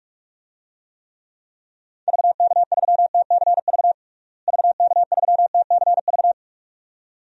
Morse code, heard as a keyed tone, about 41 wpm, 700 Hz. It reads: VK3TCV VK3TCV
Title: DXCC-100 Call Signs - 6 Characters - Repeated 2x - 45wpm